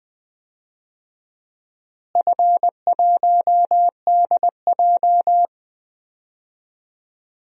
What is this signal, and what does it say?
Morse code, heard as a keyed tone, about 21 wpm, 700 Hz. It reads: F1DJ
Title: DXCC-100 Call Signs - 4 Characters - Repeated 3x - 20wpm